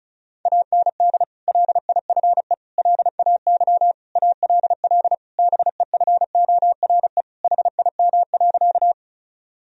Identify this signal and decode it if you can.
Morse code, heard as a keyed tone, about 34 wpm, 700 Hz. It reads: AND LIFE LAY ALL BEFORE HIM.
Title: Sentences from Top 1000 Words - Review Part 2 - 35wpm